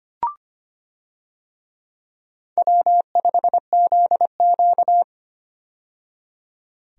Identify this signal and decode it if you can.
Morse code, heard as a keyed tone, about 25 wpm, 700 Hz. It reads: W5ZQ
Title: USA Call Signs - 4 Characters - 25wpm